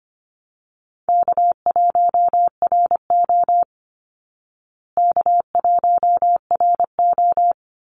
Morse code, heard as a keyed tone, about 25 wpm, 700 Hz. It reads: K1RO K1RO
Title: Call Signs Repeated 3x - 25wpm